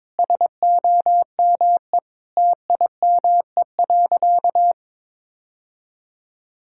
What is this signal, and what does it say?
Morse code, heard as a keyed tone, about 22 wpm, 700 Hz. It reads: SOME TIME.
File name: practice / 22wpm / Sentences with 3-Word Phrases - 22wpm